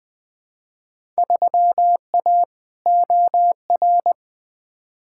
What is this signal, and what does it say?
Morse code, heard as a keyed tone, about 20 wpm, 700 Hz. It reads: 3A OR
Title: Field Day Exchanges - 20wpm